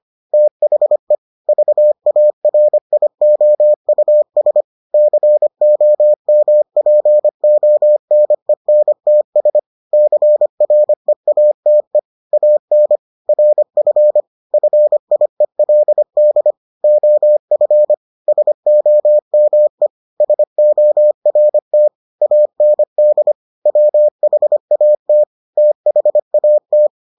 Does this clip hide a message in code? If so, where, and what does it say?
Yes: Morse code from 0.3 to 26.9 s, heard as a keyed tone, about 25 wpm, 600 Hz. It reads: THE VARIOUS COMPONENTS CREATE AN RF FIELD OF SOME SORT AND WHAT THAT